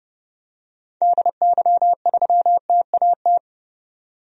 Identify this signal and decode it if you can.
Morse code, heard as a keyed tone, about 31 wpm, 700 Hz. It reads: DY3TAT
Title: DXCC-100 Call Signs - 6 Characters - 30wpm